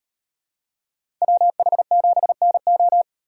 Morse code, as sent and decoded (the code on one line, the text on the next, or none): .-- .... --... -. ---
WH7NO